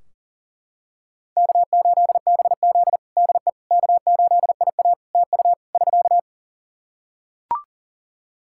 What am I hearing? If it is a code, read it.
Morse code, heard as a keyed tone, about 40 wpm, 700 Hz. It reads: K8BZ DE K8IA TU <SK>
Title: K1USN SST QSO Practice 40wpm